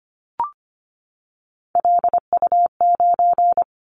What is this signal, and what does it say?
Morse code, heard as a keyed tone, about 25 wpm, 700 Hz. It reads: LU9
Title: Callsign Prefixes for Top 100 DXCC - 25wpm